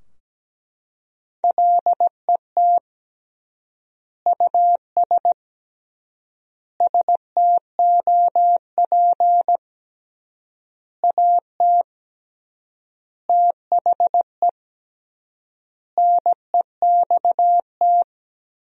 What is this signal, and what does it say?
Morse code, heard as a keyed tone, about 17 wpm, 700 Hz. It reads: LET US STOP AT THE NEXT
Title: 3x Word-Spacing - Sentences from Top 400 Words - 17wpm